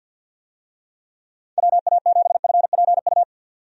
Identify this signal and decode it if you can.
Morse code, heard as a keyed tone, about 50 wpm, 700 Hz. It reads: WA7FPU
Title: USA Call Signs - 6 Characters - 50wpm